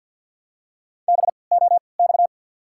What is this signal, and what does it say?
Morse code, heard as a keyed tone, about 53 wpm, spaced as for 29 wpm, 700 Hz. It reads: BOX
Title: 3x Character Spacing - Top 500 Words - 50wpm